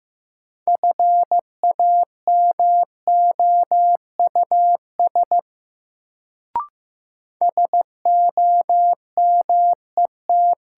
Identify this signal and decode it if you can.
Morse code, heard as a keyed tone, about 15 wpm, 700 Hz. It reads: FAMOUS SOMET
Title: Sentences from Top 1500 Words - New Word Drill Alt - 15wpm